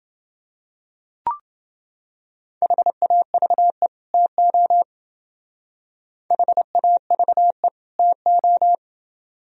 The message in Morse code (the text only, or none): HAVE TO HAVE TO